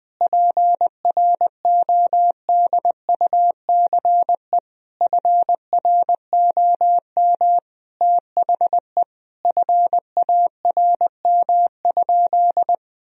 Morse code, heard as a keyed tone, about 21 wpm, 700 Hz. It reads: PRODUCE FROM THE FARM?
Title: Sentences from Top 1000 Words - Review Part 3 - 20wpm